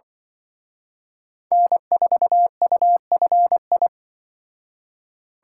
Morse code, heard as a keyed tone, about 24 wpm, 700 Hz. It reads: N4UFI
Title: Speed Racing - USA 1x3 Call Signs 20wpm